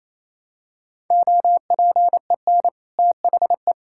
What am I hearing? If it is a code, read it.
Morse code, heard as a keyed tone, about 28 wpm, 700 Hz. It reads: OPEN THE
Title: Sentences from Top 1000 Words - Review Part 6 - 28wpm